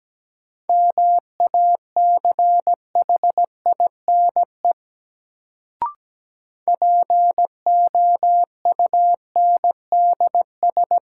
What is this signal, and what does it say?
Morse code, heard as a keyed tone, about 17 wpm, 700 Hz. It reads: MACHINE POUNDS
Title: Sentences from Top 1000 Words - Review Part 2 - New Word Drill - 17wpm